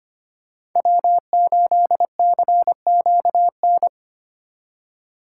Morse code, heard as a keyed tone, about 25 wpm, 700 Hz. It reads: W8CQN
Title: USA Call Signs - 5 Characters - 2x Character Spacing - 25wpm